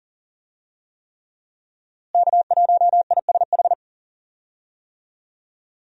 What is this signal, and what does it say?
Morse code, heard as a keyed tone, about 37 wpm, 700 Hz. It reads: K1ISH
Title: USA Call Signs - 5 Characters - Repeated 3x - 40wpm